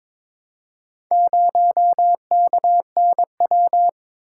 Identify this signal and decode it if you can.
Morse code, heard as a keyed tone, about 22 wpm, 700 Hz. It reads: 0KNW
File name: practice / 22wpm / Four Letter-Number Combinations - 22wpm